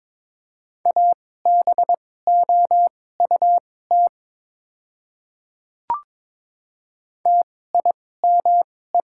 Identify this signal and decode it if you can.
Morse code, heard as a keyed tone, about 22 wpm, spaced as for 16 wpm, 700 Hz. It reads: ABOUT TIME
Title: 2x Character Spacing - Top 100 Words - 22wpm